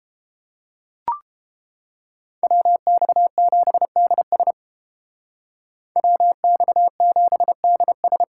Morse code, heard as a keyed tone, about 29 wpm, 700 Hz. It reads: WX7DS WX7DS